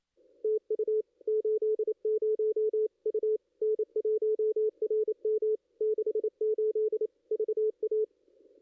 Morse code, heard as a keyed tone, about 27 wpm, 440 Hz. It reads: TU 80 U N1RM 68 VA
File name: contests / ss-cw-k5tr-op-n5rz -2200